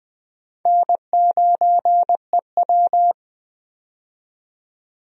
Morse code, heard as a keyed tone, about 20 wpm, 700 Hz. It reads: N9EW
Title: USA Call Signs - 4 Characters - 20wpm